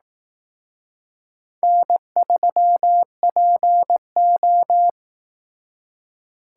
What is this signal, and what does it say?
Morse code, heard as a keyed tone, about 18 wpm, 700 Hz. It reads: N3PO
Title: Speed Racing - USA 1x2 Call Signs 15wpm